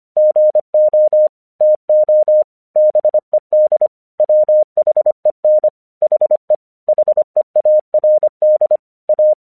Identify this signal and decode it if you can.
Morse code, heard as a keyed tone, about 25 wpm, 610 Hz. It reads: GO TO BED WHEN HE HEARD A